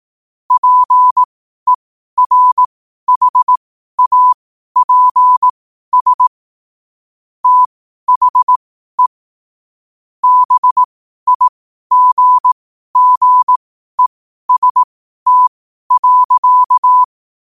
Morse code, heard as a keyed tone, about 18 wpm, spaced as for 12 wpm, 1000 Hz. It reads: PERHAPS THE BIGGEST.
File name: qotdfiles / 12wpm